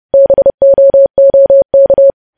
Morse code, heard as a keyed tone, about 30 wpm, 550 Hz. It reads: BOOK
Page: News Headlines in Morse Code at 30 WPM